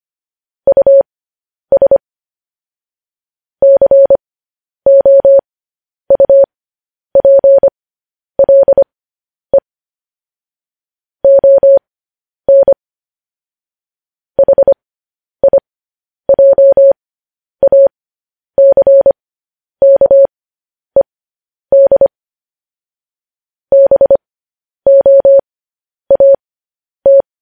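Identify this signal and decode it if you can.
Morse code, heard as a keyed tone, about 25 wpm, spaced as for 10 wpm, 550 Hz. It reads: US COUPLE ON HIJACKED BOAT
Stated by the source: Morse code practice sending news headlines at 10 wpm.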